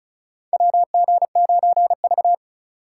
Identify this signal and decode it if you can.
Morse code, heard as a keyed tone, about 36 wpm, 700 Hz. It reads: WG9V
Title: USA Call Signs - 4 Characters - 3x Character Spacing - 35wpm